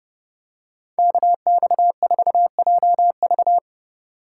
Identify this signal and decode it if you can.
Morse code, heard as a keyed tone, about 30 wpm, 700 Hz. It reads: KX4JV